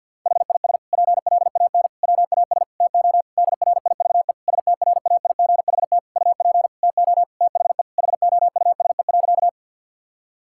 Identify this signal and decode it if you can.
Morse code, heard as a keyed tone, about 50 wpm, 700 Hz. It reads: HIS PLAN WAS TO DRIVE STRAIGHT UP TO THE HOUSE.
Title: Sentences from Top 1000 Words - Review Part 6 - 50wpm